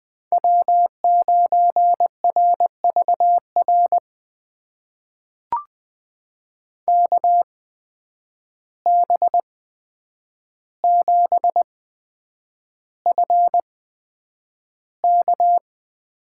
Morse code, heard as a keyed tone, about 20 wpm, spaced as for 9 wpm, 700 Hz. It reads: W9RVR KB7FK
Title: USA Call Signs - 5 Characters - 8x Character Spacing - 20wpm